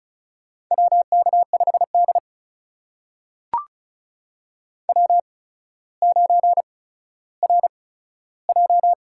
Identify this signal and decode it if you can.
Morse code, heard as a keyed tone, about 36 wpm, spaced as for 15 wpm, 700 Hz. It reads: WK5D W9RJ
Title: USA Call Signs - 4 Characters - 8x Character Spacing - 35wpm